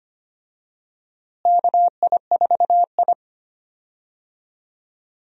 Morse code, heard as a keyed tone, about 25 wpm, 700 Hz. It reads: KI4I
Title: USA Call Signs - 4 Characters - Repeated 2x - 25wpm